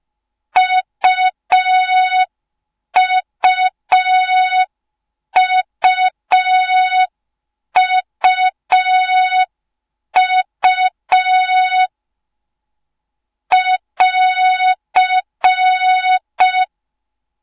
Morse code, heard as a keyed tone, about 5 wpm, 3700 Hz. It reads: UUUUU <AR>